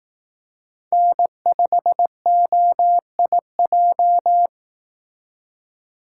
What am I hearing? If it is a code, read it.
Morse code, heard as a keyed tone, about 18 wpm, 700 Hz. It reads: N5OIJ
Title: Speed Racing - USA 1x3 Call Signs 15wpm